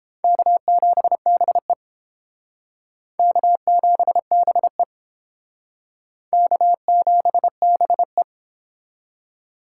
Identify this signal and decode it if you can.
Morse code, heard as a keyed tone, about 29 wpm, 700 Hz. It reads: K7BE K7BE K7BE